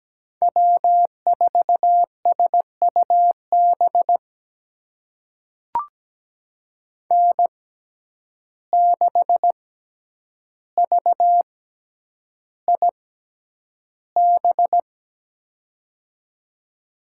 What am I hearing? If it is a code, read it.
Morse code, heard as a keyed tone, about 17 wpm, spaced as for 9 wpm, 700 Hz. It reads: W4SUB N6VIB
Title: USA Call Signs - 6x Character Spacing - 17wpm